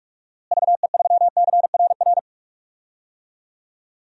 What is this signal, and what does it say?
Morse code, heard as a keyed tone, about 46 wpm, 700 Hz. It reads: VE3CRR
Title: DXCC-100 Call Signs - 6 Characters - Repeated 3x - 45wpm